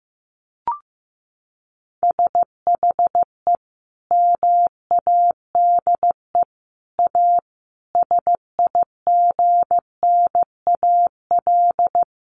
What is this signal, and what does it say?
Morse code, heard as a keyed tone, about 15 wpm, 700 Hz. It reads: SHE MADE A SIGNAL
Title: Sentences from Top 1400 Words 15wpm